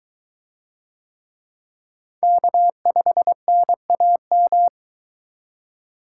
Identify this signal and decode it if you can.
Morse code, heard as a keyed tone, about 21 wpm, 700 Hz. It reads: K5NAM